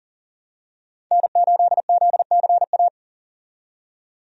Morse code, heard as a keyed tone, about 40 wpm, 700 Hz. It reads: N8ZCA
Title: USA Call Signs - 5 Characters - 40wpm